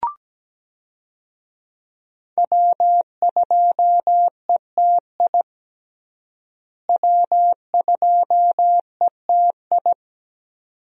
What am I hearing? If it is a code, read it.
Morse code, heard as a keyed tone, about 17 wpm, 700 Hz. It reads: W2ETI W2ETI
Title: USA Call Signs - 5 Characters - Repeated 2x - 17wpm